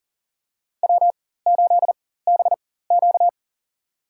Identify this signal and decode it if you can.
Morse code, heard as a keyed tone, about 40 wpm, spaced as for 19 wpm, 700 Hz. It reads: W8BQ
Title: USA Call Signs - 4x Character Spacing - 40wpm